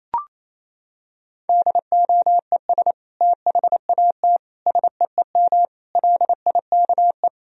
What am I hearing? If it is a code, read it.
Morse code, heard as a keyed tone, about 27 wpm, 700 Hz. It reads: DOES THAT SEEM LIKE
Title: Sentences from Top 1000 Words - Review Part 1 - 28wpm